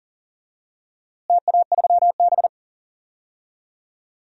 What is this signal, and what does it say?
Morse code, heard as a keyed tone, about 40 wpm, 700 Hz. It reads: TA3B
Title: DXCC-100 Call Signs - 4 Characters - Repeated 2x - 40wpm